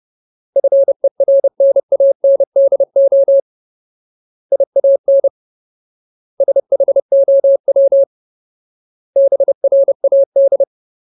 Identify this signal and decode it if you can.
Morse code, heard as a keyed tone, about 30 wpm, 550 Hz. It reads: FERNANDO IAN SHOW BRAD
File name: Ultimate Word List / 30WPM Ultimate